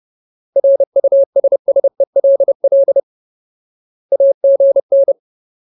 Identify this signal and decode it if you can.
Morse code, heard as a keyed tone, about 30 wpm, 550 Hz. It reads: RUSSELL AGN
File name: Ultimate Word List / 30WPM Ultimate